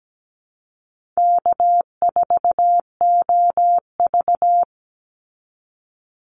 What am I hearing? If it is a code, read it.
Morse code, heard as a keyed tone, about 17 wpm, 700 Hz. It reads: K4OV
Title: Easy Call Signs Repeated 3x - 17wpm